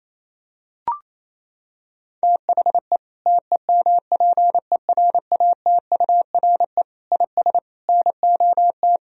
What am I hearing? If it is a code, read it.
Morse code, heard as a keyed tone, about 28 wpm, 700 Hz. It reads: THE TEMPERATURE IS NOT
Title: Sentences from Top 1000 Words - Review Part 4 - 28wpm